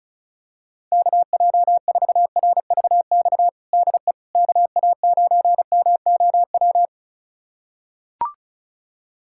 Morse code, heard as a keyed tone, about 34 wpm, 700 Hz. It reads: KJ4RVX DE KA9MOW
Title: USA Callsign DE Callsign - 6 Characters - 35wpm